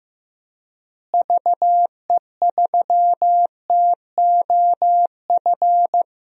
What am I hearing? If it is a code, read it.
Morse code, heard as a keyed tone, about 15 wpm, 700 Hz. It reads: VE3TOF
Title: DXCC-100 Call Signs - 6 Characters - Repeated 3x - 15wpm